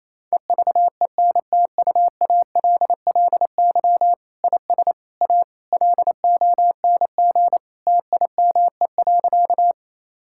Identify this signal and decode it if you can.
Morse code, heard as a keyed tone, about 28 wpm, 700 Hz. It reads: EVENTUALLY IS A LONG TIME.
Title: Sentences from Top 1000 Words - Review Part 6 - 28wpm